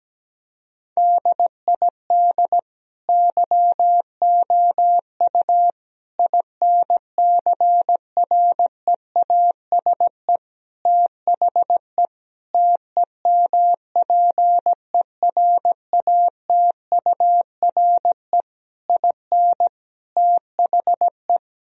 Morse code, heard as a keyed tone, about 17 wpm, 700 Hz. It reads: DID YOU INCREASE THE TEMPERATURE IN THE
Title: Sentences from Top 1000 Words - Review Part 4 - 17wpm